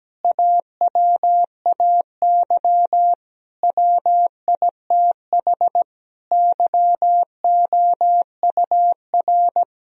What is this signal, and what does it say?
Morse code, heard as a keyed tone, about 17 wpm, 700 Hz. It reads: AWAY WITH YOUR
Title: Sentences from Top 1000 Words - Review Part 1 - 17wpm